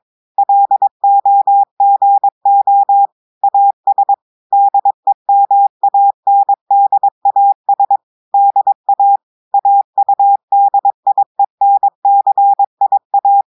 Morse code, heard as a keyed tone, about 22 wpm, 800 Hz. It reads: LOGO AS DEMANDAS DA AUDIENCIA
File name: audios / portugais / portugais texte en clair 2 vitesse 22 mots minutes